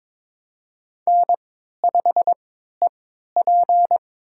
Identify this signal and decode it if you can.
Morse code, heard as a keyed tone, about 22 wpm, spaced as for 13 wpm, 700 Hz. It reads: N5EP
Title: USA Call Signs - 4 Characters - 3x Character Spacing - 22wpm